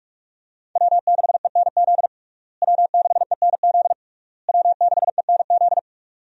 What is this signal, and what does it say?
Morse code, heard as a keyed tone, about 46 wpm, 700 Hz. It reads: W6ENZ W6ENZ W6ENZ
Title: USA Call Signs - 5 Characters - Repeated 3x - 45wpm